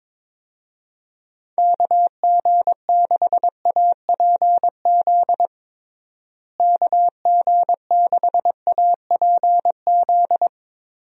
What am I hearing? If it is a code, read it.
Morse code, heard as a keyed tone, about 22 wpm, 700 Hz. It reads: KG6APZ KG6APZ
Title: USA Call Signs - 6 Characters - Repeated 3x - 22wpm